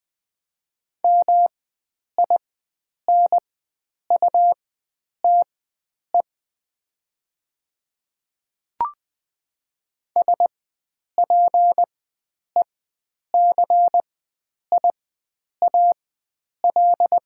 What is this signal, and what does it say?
Morse code, heard as a keyed tone, about 20 wpm, spaced as for 9 wpm, 700 Hz. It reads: MINUTE SPECIAL
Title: 4x Character Spacing - Top 500 Words - 20wpm